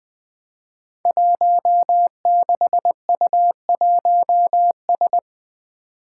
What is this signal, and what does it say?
Morse code, heard as a keyed tone, about 21 wpm, 700 Hz. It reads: 16U1S
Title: Five Challenging Letter-Number Combinations 20wpm